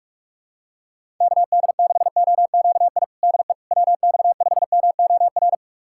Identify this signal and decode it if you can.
Morse code, heard as a keyed tone, about 46 wpm, 690 Hz. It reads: KD6OQI DE WX5MOR
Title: USA Callsign DE Callsign - 6 Characters - 45wpm